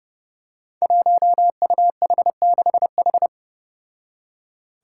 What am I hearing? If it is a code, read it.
Morse code, heard as a keyed tone, about 32 wpm, 700 Hz. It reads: 1UH6H
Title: Five Challenging Letter-Number Combinations 30wpm